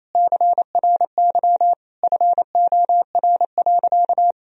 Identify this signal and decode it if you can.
Morse code, heard as a keyed tone, about 28 wpm, 700 Hz. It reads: CRY FOR.
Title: Sentences from Top 1000 Words - Review Part 2 - 28wpm